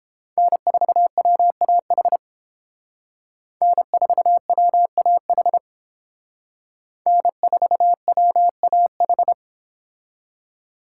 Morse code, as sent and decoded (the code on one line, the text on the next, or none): -. ....- .-- .- .... / -. ....- .-- .- .... / -. ....- .-- .- ....
N4WAH N4WAH N4WAH